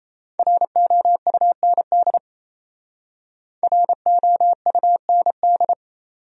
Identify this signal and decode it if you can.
Morse code, heard as a keyed tone, about 34 wpm, 700 Hz. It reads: ROUND ROUND
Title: Speed-Racing Top 300 Words 28wpm